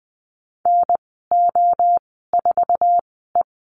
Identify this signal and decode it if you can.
Morse code, heard as a keyed tone, about 20 wpm, spaced as for 15 wpm, 700 Hz. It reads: NO4E
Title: 076 Lesson 13g - Letter L - Call Signs 20wpm